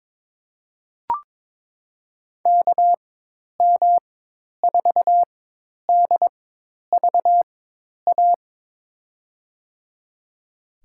Morse code, heard as a keyed tone, about 22 wpm, spaced as for 10 wpm, 700 Hz. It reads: KM4DVA
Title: USA Call Signs - 4x Character Spacing - 22wpm